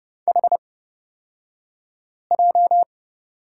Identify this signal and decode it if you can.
Morse code, heard as a keyed tone, about 30 wpm, spaced as for 5 wpm, 700 Hz. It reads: HJ
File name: Lesson 38d - Letter q - Two Characters 30wpm